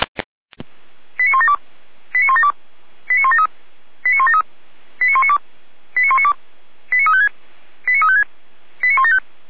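Five-tone selective call (ZVEI), heard as quick runs of five tones, tones of about 70 ms. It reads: 89172 89172 89173 89173 89192 89192 89356 89356 89175
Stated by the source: Received on SDR by feuerwehr-orlamuende.org.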